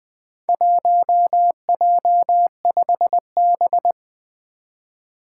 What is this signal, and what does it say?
Morse code, heard as a keyed tone, about 20 wpm, 700 Hz. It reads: 1J5B
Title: Four Challenging Letter-Number Combinations 20wpm